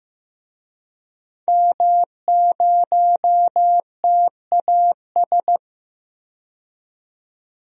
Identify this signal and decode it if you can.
Morse code, heard as a keyed tone, about 15 wpm, 700 Hz. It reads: M0TAS
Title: DXCC-100 Call Signs - 5 Characters - 15wpm